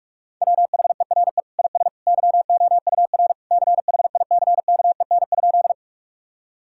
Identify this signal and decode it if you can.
Morse code, heard as a keyed tone, about 46 wpm, 700 Hz. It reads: WHERE IS YOUR CHICKEN?
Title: Sentences from Top 1000 Words - Review Part 5 - 45wpm